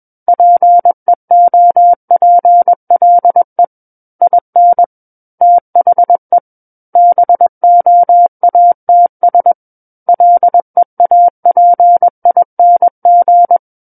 Morse code, heard as a keyed tone, about 21 wpm, 700 Hz. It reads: PEOPLE IN THE BOATS LEAPING
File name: Book One / 12-Chapter 12 - 21 WPM